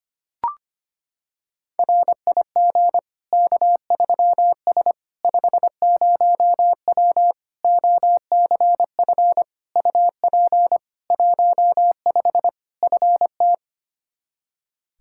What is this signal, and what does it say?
Morse code, heard as a keyed tone, about 25 wpm, 700 Hz. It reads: RIG K3S 50W OCF UP 15 FT
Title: Practice QSO - Exchange 2 - Info - 25wpm